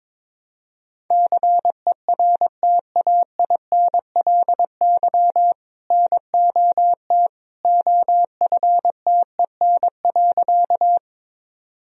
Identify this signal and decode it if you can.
Morse code, heard as a keyed tone, about 22 wpm, 700 Hz. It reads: CERTAINLY NOT OFTEN.
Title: Sentences from Top 1000 Words - Review Part 1 - 22wpm